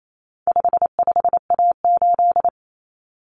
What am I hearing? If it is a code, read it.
Morse code, heard as a keyed tone, about 28 wpm, 700 Hz. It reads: 55A8
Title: Common Cut Numbers - 4-Digit 28wpm